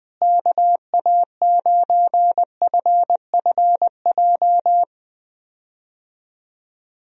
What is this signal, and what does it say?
Morse code, heard as a keyed tone, about 20 wpm, 700 Hz. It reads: KA9FFJ
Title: USA Call Signs - 6 Characters - Repeated 3x - 20wpm